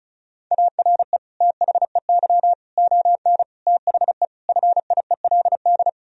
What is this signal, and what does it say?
Morse code, heard as a keyed tone, about 36 wpm, 700 Hz. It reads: ARE THEY ON THE FIELD
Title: Sentences from Top 1000 Words - Review Part 2 - 35wpm